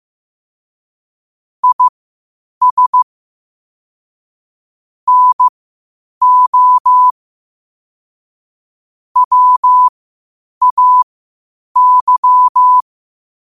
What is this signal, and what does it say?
Morse code, heard as a keyed tone, about 15 wpm, spaced as for 8 wpm, 1000 Hz. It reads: IS NO WAY
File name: qotdfiles / 8wpm